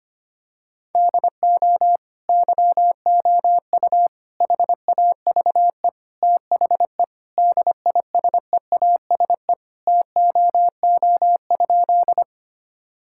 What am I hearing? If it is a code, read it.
Morse code, heard as a keyed tone, about 25 wpm, 700 Hz. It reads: DO YOU HAVE THE DISEASE TOO?
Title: Sentences from Top 1000 Words - Review Part 6 - 25wpm